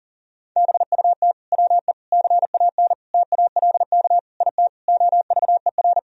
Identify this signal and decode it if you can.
Morse code, heard as a keyed tone, about 40 wpm, 700 Hz. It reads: BUT WE CAN TALK IT OVER
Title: Sentences from Top 1000 Words - Review Part 4 - 40wpm